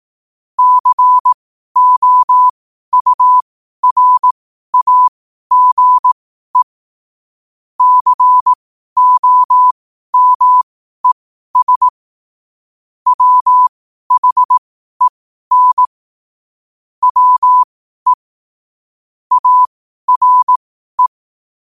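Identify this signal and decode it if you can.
Morse code, heard as a keyed tone, about 18 wpm, spaced as for 12 wpm, 1000 Hz. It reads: COURAGE COMES WHEN WE ARE
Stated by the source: Morse code quote-of-the-day practice at 12 words per minute.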